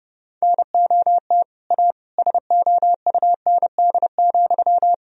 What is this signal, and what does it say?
Morse code, heard as a keyed tone, about 30 wpm, 700 Hz. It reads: NOT A SOUND,
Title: Sentences from Top 1000 Words - Review Part 1 - 30wpm